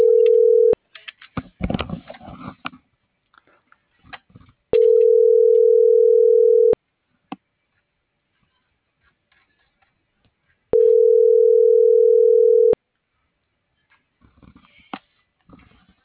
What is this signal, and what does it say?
Ringback tone (North America), 440 + 480 Hz.